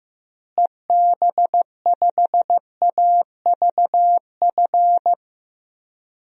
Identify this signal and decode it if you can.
Morse code, heard as a keyed tone, about 15 wpm, 700 Hz. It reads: EB5AVF
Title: DXCC-100 Call Signs - 6 Characters - Repeated 2x - 15wpm